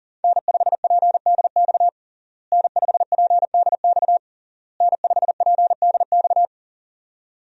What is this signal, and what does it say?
Morse code, heard as a keyed tone, about 40 wpm, 700 Hz. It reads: N5PDX N5PDX N5PDX
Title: USA Call Signs - 5 Characters - Repeated 3x - 40wpm